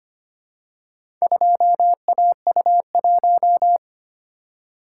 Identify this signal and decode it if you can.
Morse code, heard as a keyed tone, about 25 wpm, 700 Hz. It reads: 2AU1
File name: practice / 25wpm / Four Letter-Number Combinations - 25wpm